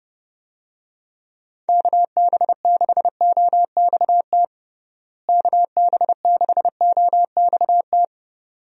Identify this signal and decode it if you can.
Morse code, heard as a keyed tone, about 30 wpm, 700 Hz. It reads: KB6OXT KB6OXT
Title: USA Call Signs - 6 Characters - Repeated 2x - 30wpm